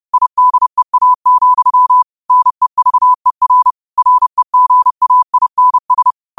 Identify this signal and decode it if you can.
Morse code, heard as a keyed tone, about 30 wpm, 1000 Hz. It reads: IDEA, NEVER REGAINS
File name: qotdfiles / 30wpm